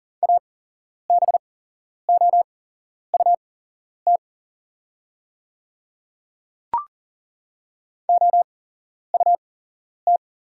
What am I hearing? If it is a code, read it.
Morse code, heard as a keyed tone, about 39 wpm, spaced as for 11 wpm, 700 Hz. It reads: ABOUT OUT
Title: 8x Character Spacing - Top 100 Words - 40wpm